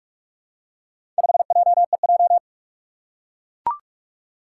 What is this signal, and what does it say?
Morse code, heard as a keyed tone, about 41 wpm, 700 Hz. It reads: 5JEJ
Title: Four Challenging Letter-Number Combinations 45wpm